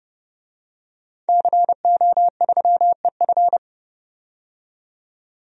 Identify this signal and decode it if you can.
Morse code, heard as a keyed tone, about 30 wpm, 700 Hz. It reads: CO3EF
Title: DXCC-100 Call Signs - 5 Characters - 30wpm